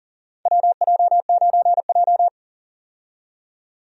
Morse code, heard as a keyed tone, about 40 wpm, 700 Hz. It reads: WJ9J
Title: USA Call Signs - 4 Characters - 40wpm